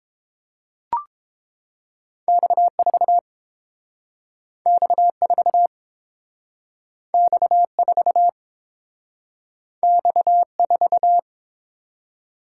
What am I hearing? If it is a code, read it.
Morse code, heard as a keyed tone, about 30 wpm, 700 Hz. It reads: X4 X4 X4 X4